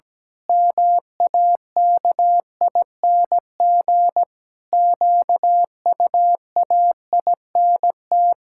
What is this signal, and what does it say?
Morse code, heard as a keyed tone, about 17 wpm, 700 Hz. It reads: MAKING QUAINT
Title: Pangram Challenge v2 - 17wpm